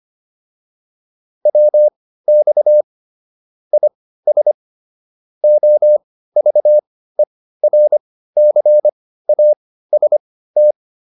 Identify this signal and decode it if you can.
Morse code, heard as a keyed tone, about 25 wpm, spaced as for 15 wpm, 600 Hz. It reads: WX IS OVERCAST
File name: QSO307 15